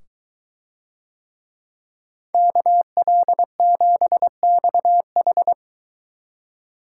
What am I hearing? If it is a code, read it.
Morse code, heard as a keyed tone, about 22 wpm, 700 Hz. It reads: KL7XH